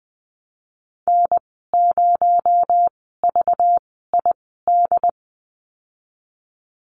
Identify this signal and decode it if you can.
Morse code, heard as a keyed tone, about 20 wpm, spaced as for 15 wpm, 700 Hz. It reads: N0VID
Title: 238 Lesson 38g - Letter q - Call Signs 20wpm